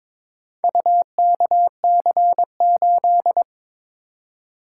Morse code, heard as a keyed tone, about 22 wpm, 700 Hz. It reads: UKC8
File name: practice / 22wpm / Four Letter-Number Combinations - 22wpm